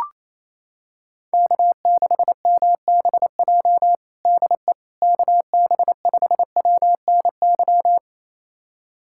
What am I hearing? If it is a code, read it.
Morse code, heard as a keyed tone, about 28 wpm, 700 Hz. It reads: K6MBJ DE KB5WNY
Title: USA Callsign DE Callsign 28wpm